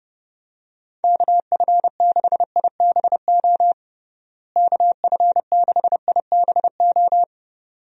Morse code, heard as a keyed tone, about 30 wpm, 700 Hz. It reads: KF6IBO KF6IBO
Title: USA Call Signs - 6 Characters - Repeated 2x - 30wpm